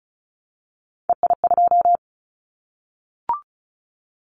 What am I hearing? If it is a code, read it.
Morse code, heard as a keyed tone, about 36 wpm, 700 Hz. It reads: EI2
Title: Callsign Prefixes for Top 50 DXCC - 35wpm